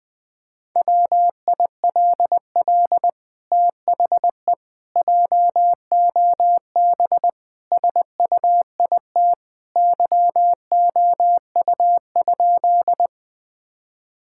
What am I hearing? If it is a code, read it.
Morse code, heard as a keyed tone, about 20 wpm, 700 Hz. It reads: WILL THE JOB SUIT YOU?
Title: Sentences from Top 1600 Words 20wpm